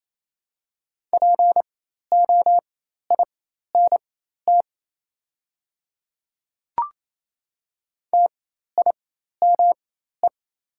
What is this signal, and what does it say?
Morse code, heard as a keyed tone, about 28 wpm, spaced as for 13 wpm, 700 Hz. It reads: POINT TIME
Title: 4x Character Spacing - Top 200 Words - 28wpm